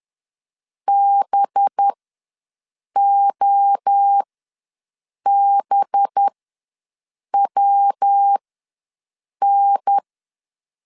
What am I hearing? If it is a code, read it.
Morse code, heard as a keyed tone, about 10 wpm, spaced as for 6 wpm, 800 Hz. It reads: BOBWN
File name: LEZIONI / 5 LBVC / 5 LB ws 10wpm 3616